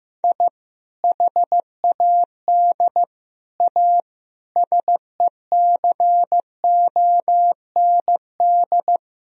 Morse code, heard as a keyed tone, about 15 wpm, 700 Hz. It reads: I HAD A SECOND
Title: Sentences from Top 1000 Words - Review Part 1 - 15wpm